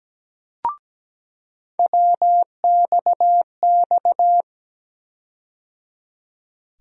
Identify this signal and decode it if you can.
Morse code, heard as a keyed tone, about 17 wpm, 700 Hz. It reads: WXX
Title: USA Call Sign Postfixes - 3 Letters 17wpm